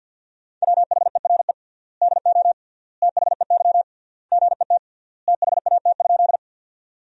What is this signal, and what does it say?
Morse code, heard as a keyed tone, about 53 wpm, 700 Hz. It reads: WHERE DO THEY GET THAT?
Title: 3x Word-Spacing - Sentences from Top 200 Words - 50wpm